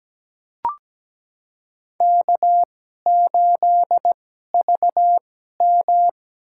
Morse code, heard as a keyed tone, about 17 wpm, spaced as for 12 wpm, 700 Hz. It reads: K8VM
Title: USA Call Signs - 4 Characters - 2x Character Spacing - 17wpm